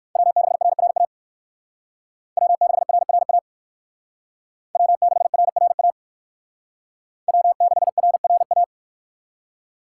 Morse code, heard as a keyed tone, about 57 wpm, 700 Hz. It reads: W6RRA W6RRA W6RRA W6RRA